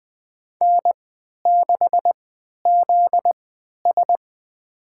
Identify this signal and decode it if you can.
Morse code, heard as a keyed tone, about 20 wpm, spaced as for 11 wpm, 700 Hz. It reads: N6ZS
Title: USA Call Signs - 4 Characters - 3x Character Spacing - 20wpm